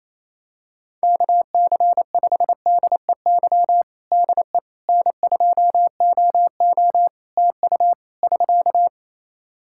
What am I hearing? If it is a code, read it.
Morse code, heard as a keyed tone, about 28 wpm, 700 Hz. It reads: KC5DEY DE N2OO TU <SK>
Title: K1USN SST QSO Practice 28wpm